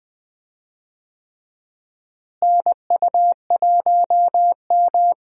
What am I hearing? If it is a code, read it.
Morse code, heard as a keyed tone, about 20 wpm, 700 Hz. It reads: NU1M
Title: USA Call Signs - 4 Characters - Repeated 3x - 20wpm